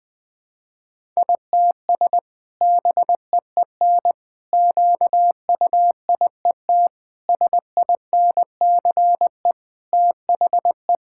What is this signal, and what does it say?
Morse code, heard as a keyed tone, about 21 wpm, 700 Hz. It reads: ITS BEEN QUIET SINCE THE
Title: Sentences with 2-Word Phrases - Spoken Beginning - 20wpm